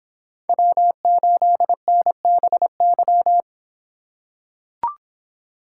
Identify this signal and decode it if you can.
Morse code, heard as a keyed tone, about 26 wpm, 700 Hz. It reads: W8NBY